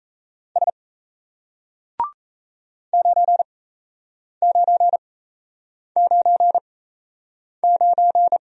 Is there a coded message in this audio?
Yes: Morse code from 0.6 to 8.4 s, heard as a keyed tone, about 38 wpm, spaced as for 8 wpm, 700 Hz. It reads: S 9999